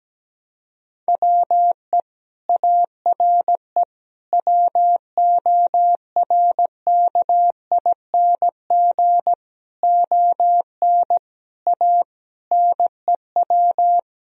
Morse code, heard as a keyed tone, about 17 wpm, 700 Hz. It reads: WE ARE WORKING ON A NEW
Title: Sentences from Top 1000 Words - Review Part 4 - 17wpm